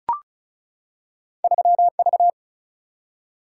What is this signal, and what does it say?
Morse code, heard as a keyed tone, about 36 wpm, 700 Hz. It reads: 3V
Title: Single-Letter Single-Number 35wpm